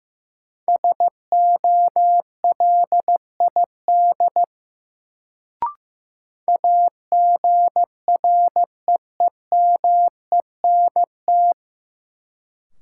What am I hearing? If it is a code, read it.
Morse code, heard as a keyed tone, about 15 wpm, 700 Hz. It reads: SOLID AGREEMENT
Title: Sentences from Top 1500 Words - New Word Drill - 15wpm